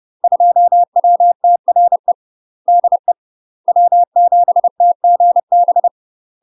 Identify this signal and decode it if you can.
Morse code, heard as a keyed tone, about 30 wpm, 700 Hz. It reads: 2WTRE DE W7TGB
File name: UK MASTER QSOs TRAINING / 30WPM / UK MASTER QSOs 0039